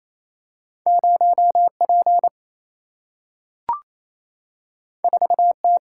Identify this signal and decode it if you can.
Morse code, heard as a keyed tone, about 28 wpm, 700 Hz. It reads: 0 P 4 T
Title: Single-Letter Single-Number 28wpm